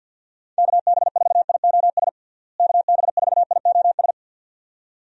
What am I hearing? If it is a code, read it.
Morse code, heard as a keyed tone, about 51 wpm, 700 Hz. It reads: KB4IOS KB4IOS
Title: USA Call Signs - 6 Characters - Repeated 2x - 50wpm